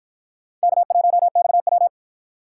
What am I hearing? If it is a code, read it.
Morse code, heard as a keyed tone, about 53 wpm, 700 Hz. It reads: K1XW